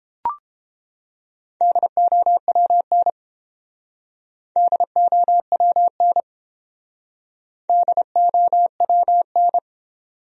Morse code, heard as a keyed tone, about 29 wpm, 700 Hz. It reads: DOWN DOWN DOWN